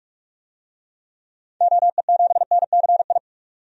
Morse code, heard as a keyed tone, about 42 wpm, 700 Hz. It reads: OE7NCI
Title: DXCC-100 Call Signs - 6 Characters - 45wpm